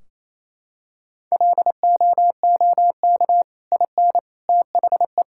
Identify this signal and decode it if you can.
Morse code, heard as a keyed tone, about 28 wpm, 700 Hz. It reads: LOOK IN THE
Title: Sentences from Top 1600 Words 28wpm